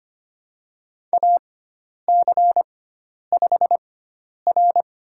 Morse code, heard as a keyed tone, about 25 wpm, spaced as for 10 wpm, 700 Hz. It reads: AC5R
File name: Lesson 16g - Letter c - Call Signs 25wpm